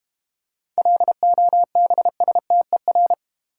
Morse code, heard as a keyed tone, about 32 wpm, 700 Hz. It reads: LOBSTER